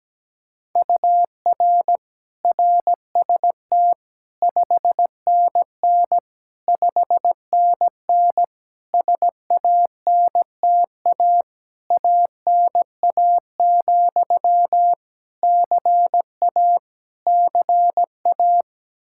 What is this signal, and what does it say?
Morse code, heard as a keyed tone, about 17 wpm, 700 Hz. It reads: UR RST 5NN 5NN SANTA ANA, CA CA
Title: Practice QSO - Exchange 1 - Beginning - 17wpm